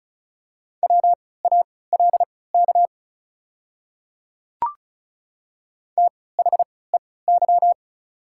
Morse code, heard as a keyed tone, about 34 wpm, spaced as for 20 wpm, 700 Hz. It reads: WALK THEY
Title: 3x Character Spacing - Top 300 Words - 35wpm